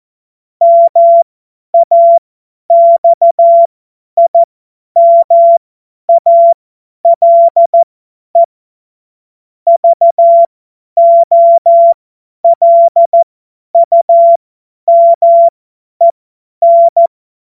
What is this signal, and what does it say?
Morse code, heard as a keyed tone, about 14 wpm, spaced as for 10 wpm, 680 Hz. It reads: MAXIMALE VOLUMEN
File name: Morsekurs / Witze / Witz 04. Mai 2021, 50 F70